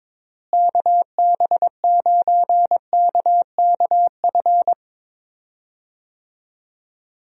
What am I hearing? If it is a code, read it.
Morse code, heard as a keyed tone, about 22 wpm, 700 Hz. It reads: KB9KKF
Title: USA Call Signs - 6 Characters - Repeated 2x - 22wpm